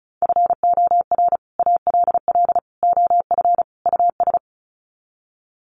Morse code, heard as a keyed tone, about 34 wpm, 700 Hz. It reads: FOR ALL OF US
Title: Sets of 4 Words 35wpm